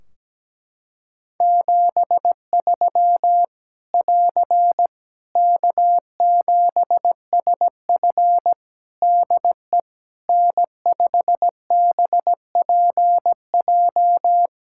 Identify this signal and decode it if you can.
Morse code, heard as a keyed tone, about 17 wpm, 700 Hz. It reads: 73 <AR> K7SF DE N5BPJ
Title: Practice QSO - Exchange 4 - Ending - 17wpm